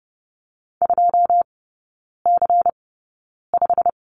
Morse code, heard as a keyed tone, about 30 wpm, spaced as for 9 wpm, 700 Hz. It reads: 2C5
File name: Lesson 16e - Letter c - Three Characters 30wpm